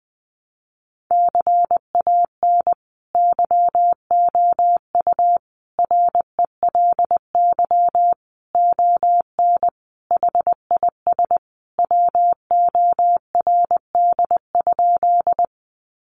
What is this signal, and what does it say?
Morse code, heard as a keyed tone, about 20 wpm, 700 Hz. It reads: CAN YOU RELY ON HIS WORD?
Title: Sentences from Top 1200 Words 20wpm